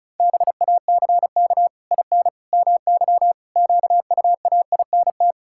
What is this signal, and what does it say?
Morse code, heard as a keyed tone, about 36 wpm, 700 Hz. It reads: BACK IN MY QUAINT
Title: Pangram Challenge v2 - 35wpm